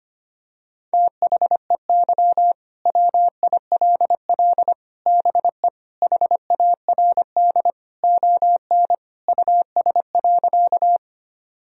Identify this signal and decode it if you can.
Morse code, heard as a keyed tone, about 25 wpm, 700 Hz. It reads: THEY WILL BE HARD ON US.
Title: Sentences from Top 1000 Words - Review Part 1 - 25wpm